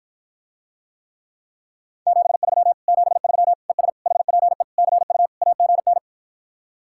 Morse code, heard as a keyed tone, about 54 wpm, 700 Hz. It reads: 73 73 ES HPE CU AGN